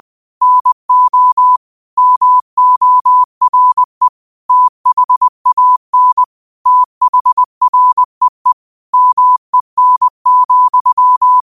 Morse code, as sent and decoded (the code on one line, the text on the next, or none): -. --- / -- --- .-. / - .... .- -. / - .... .-. . / -- . -. --..--
NO MORE THAN THREE MEN,